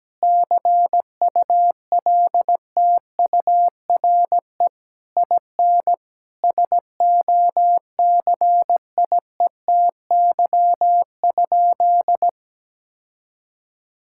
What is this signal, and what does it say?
Morse code, heard as a keyed tone, about 17 wpm, 700 Hz. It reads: CULTURE IN SOCIETY?
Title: Sentences from Top 1000 Words - Review Part 6 - 17wpm